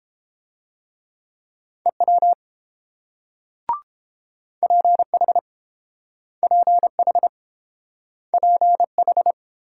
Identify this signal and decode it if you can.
Morse code, heard as a keyed tone, about 30 wpm, 700 Hz. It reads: EW PH PH PH